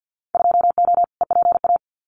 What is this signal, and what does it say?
Morse code, heard as a keyed tone, about 50 wpm, 700 Hz. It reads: 2O EPA
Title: Winter Field Day Exchanges Encore 50wpm